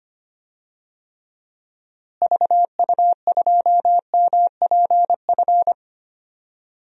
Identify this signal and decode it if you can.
Morse code, heard as a keyed tone, about 25 wpm, 700 Hz. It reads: VU2MPF
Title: DXCC-100 Call Signs - 6 Characters - Repeated 2x - 25wpm